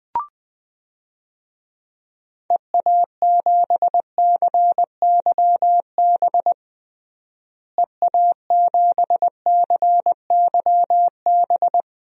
Morse code, heard as a keyed tone, about 20 wpm, 700 Hz. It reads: EA7CYB EA7CYB
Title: DXCC-100 Call Signs - 6 Characters - Repeated 3x - 20wpm